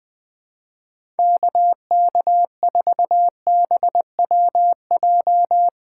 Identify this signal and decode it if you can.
Morse code, heard as a keyed tone, about 20 wpm, 700 Hz. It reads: KK4BWJ
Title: USA Callsign DE Callsign - 6 Characters - 20wpm